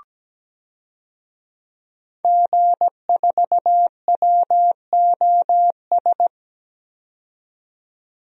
Morse code, heard as a keyed tone, about 17 wpm, 700 Hz. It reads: G4WOS
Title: DXCC-100 Call Signs - 5 Characters - 17wpm